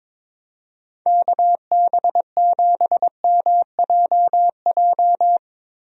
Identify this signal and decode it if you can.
Morse code, heard as a keyed tone, about 22 wpm, 700 Hz. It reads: KB7MJJ
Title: USA Call Signs - 6 Characters - Repeated 3x - 22wpm